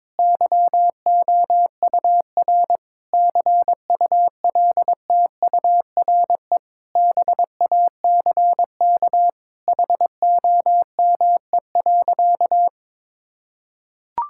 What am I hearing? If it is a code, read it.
Morse code, heard as a keyed tone, about 21 wpm, 700 Hz. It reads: YOUR CULTURE BACK HOME.
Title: Sentences from Top 1000 Words - Review Part 6 - 22wpm